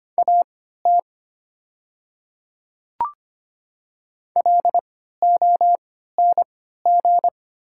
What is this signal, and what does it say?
Morse code, heard as a keyed tone, about 25 wpm, spaced as for 14 wpm, 700 Hz. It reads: AT LONG
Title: 3x Character Spacing - Top 200 Words - 25wpm